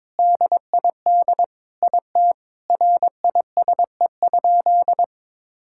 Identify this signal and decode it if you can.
Morse code, heard as a keyed tone, about 22 wpm, 700 Hz. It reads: DID IT RISE?
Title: Sentences from Top 1000 Words - Review Part 6 - 22wpm